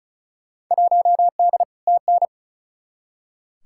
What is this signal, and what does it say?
Morse code, heard as a keyed tone, about 34 wpm, 700 Hz. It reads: 1D TN
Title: Field Day Exchanges - 35wpm